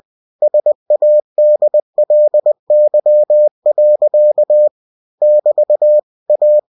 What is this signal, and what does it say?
Morse code, heard as a keyed tone, about 20 wpm, 600 Hz. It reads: SADLY. <BT> A
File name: cw-books / winnie-the-pooh / winnie-20wpm-page-0030